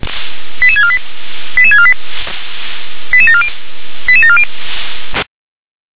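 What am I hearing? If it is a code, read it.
Five-tone selective call (ZVEI), heard as quick runs of five tones, tones of about 70 ms. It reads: 88648 88648 88740 88740
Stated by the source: ZVEI five-tone call 88081 received on an SDR.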